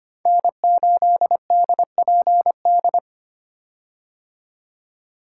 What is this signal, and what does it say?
Morse code, heard as a keyed tone, about 25 wpm, 700 Hz. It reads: N8DPD
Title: K1USN SST QSO Practice - 25wpm